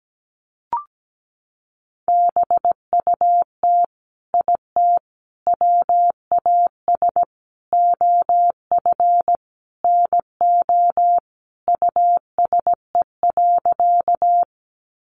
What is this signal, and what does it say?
Morse code, heard as a keyed tone, about 17 wpm, 700 Hz. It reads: BUT IT WAS OF NO USE.
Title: Sentences from Top 100 Words 17wpm